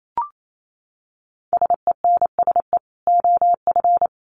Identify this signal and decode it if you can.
Morse code, heard as a keyed tone, about 28 wpm, 700 Hz. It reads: SENSE OF
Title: Dit Challenge 28wpm